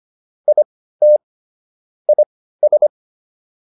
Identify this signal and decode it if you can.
Morse code, heard as a keyed tone, about 25 wpm, spaced as for 15 wpm, 600 Hz. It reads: IT IS